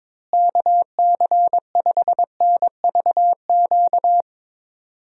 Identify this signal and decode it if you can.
Morse code, heard as a keyed tone, about 22 wpm, 700 Hz. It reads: KC5NVQ
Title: USA Call Signs - 6 Characters - Repeated 2x - 22wpm